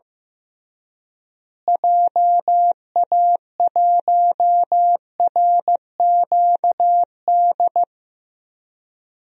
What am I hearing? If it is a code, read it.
Morse code, heard as a keyed tone, about 15 wpm, 700 Hz. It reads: JA1RQD
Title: DXCC-100 Call Signs - 6 Characters - Repeated 3x - 15wpm